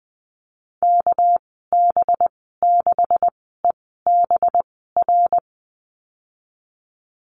Morse code, heard as a keyed tone, about 20 wpm, spaced as for 15 wpm, 700 Hz. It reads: KB6EBR
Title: 213 Lesson 34g - Number 8 - Call Signs 20wpm